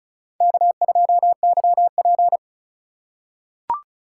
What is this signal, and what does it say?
Morse code, heard as a keyed tone, about 36 wpm, 700 Hz. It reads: K2YP
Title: USA Call Signs - 4 Characters - 3x Character Spacing - 35wpm